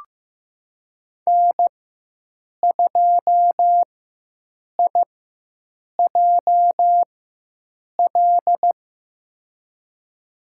Morse code, heard as a keyed tone, about 16 wpm, spaced as for 7 wpm, 700 Hz. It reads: N2IJL
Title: USA Call Signs - 5 Characters - 4x Character Spacing - 15wpm